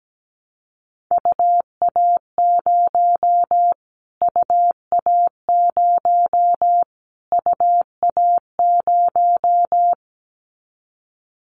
Morse code, heard as a keyed tone, about 17 wpm, 700 Hz. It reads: UA0 UA0 UA0
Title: Callsign Prefixes for Top 25 DXCC - Repeated 3x - 17wpm